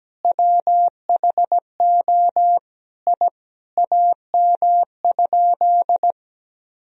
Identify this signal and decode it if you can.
Morse code, heard as a keyed tone, about 17 wpm, 700 Hz. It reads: WHO I AM?
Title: Sentences from Top 1000 Words - Review Part 2 - 17wpm